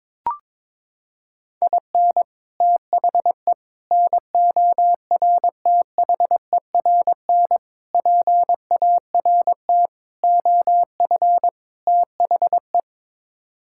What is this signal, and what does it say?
Morse code, heard as a keyed tone, about 22 wpm, 700 Hz. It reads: IN THE NORTHERN PART OF THE
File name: practice / 22wpm / Sets of 6 Words - 22wpm